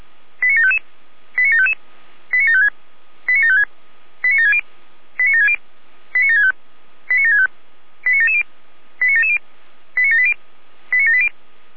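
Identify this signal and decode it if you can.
Five-tone selective call (ZVEI), heard as quick runs of five tones, tones of about 70 ms. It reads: 89755 89755 89756 89756 89760 89760 89765 89765 89770 89770 89780 89780
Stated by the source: ZVEI five-tone selective call 89590, received via SDR.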